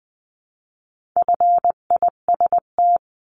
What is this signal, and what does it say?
Morse code, heard as a keyed tone, about 19 wpm, 700 Hz. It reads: FIST
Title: 259 Lesson 42d - Words 18-20wpm